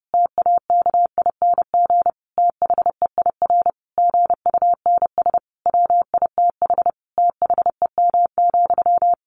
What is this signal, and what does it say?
Morse code, heard as a keyed tone, about 30 wpm, 700 Hz. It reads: TAKING THEIR GUNS WITH THEM,
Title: A Sound Of Thunder 30wpm